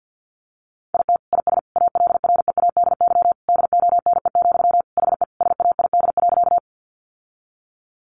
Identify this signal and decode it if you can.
Morse code, heard as a keyed tone, about 50 wpm, 700 Hz. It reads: IT IS ALREADY DONE, HE SAID.
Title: Sentences from Top 700 Words 50wpm